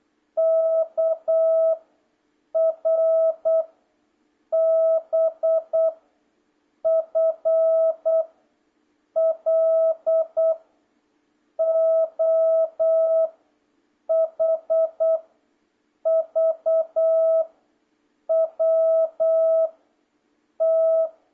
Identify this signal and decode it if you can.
Morse code, heard as a keyed tone, about 8 wpm, 660 Hz. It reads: K R B F L O H V W T